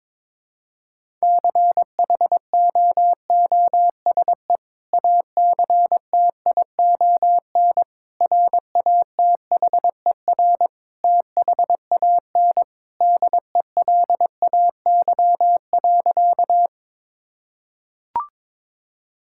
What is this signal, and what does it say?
Morse code, heard as a keyed tone, about 22 wpm, 700 Hz. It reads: CHOOSE ACTION RATHER THAN DELAY.
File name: practice / 22wpm / Sentences with 2-Word Phrases - 22wpm